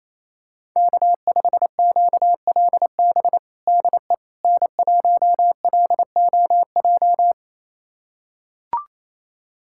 Morse code, heard as a keyed tone, about 28 wpm, 700 Hz. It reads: K5QLB DE N1LOJ
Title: USA Callsign DE Callsign - 5 Characters - 28wpm